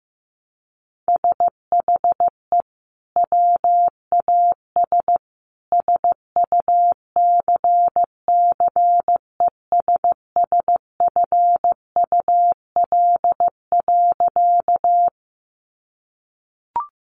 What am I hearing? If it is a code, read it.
Morse code, heard as a keyed tone, about 15 wpm, 700 Hz. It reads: SHE WAS SUCCESSFUL.
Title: Sentences from Top 900 Words 15wpm